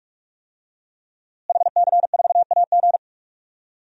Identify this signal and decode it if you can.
Morse code, heard as a keyed tone, about 41 wpm, 700 Hz. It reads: HC4AG
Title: DXCC-100 Call Signs - 5 Characters - Repeated 2x - 45wpm